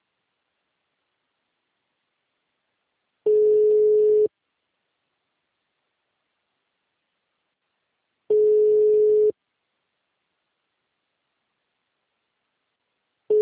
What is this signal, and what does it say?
Ringback tone (Europe, 425 Hz).
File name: out- -2043- - - .36632